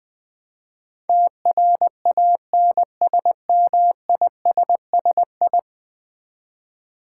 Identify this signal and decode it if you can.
Morse code, heard as a keyed tone, about 20 wpm, 700 Hz. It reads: TRANSMISSI
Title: Build A Word - 5 Letters At A Time - 12-Letter Words - 20wpm